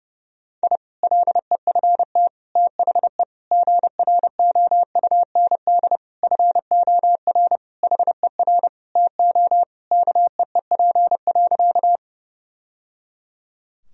Morse code, heard as a keyed tone, about 30 wpm, 700 Hz. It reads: I LEFT THE GROUND FOR HER TO KEEP.
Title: Sentences from Top 1000 Words - Review Part 2 - 30wpm